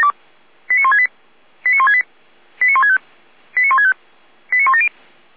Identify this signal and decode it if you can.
Five-tone selective call (ZVEI), heard as quick runs of five tones, tones of about 70 ms. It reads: 89168 89168 89165 89165 89170